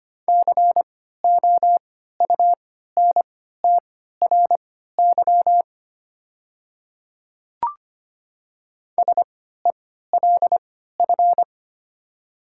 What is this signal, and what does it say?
Morse code, heard as a keyed tone, about 25 wpm, spaced as for 14 wpm, 700 Hz. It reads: COUNTRY SELF
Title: 3x Character Spacing - Top 300 Words - 25wpm